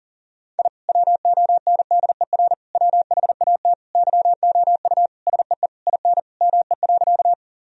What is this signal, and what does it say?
Morse code, heard as a keyed tone, about 40 wpm, 700 Hz. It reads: I WONDER WHAT YOU SEE IN ME.
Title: Sentences from Top 1000 Words - Review Part 3 - 40wpm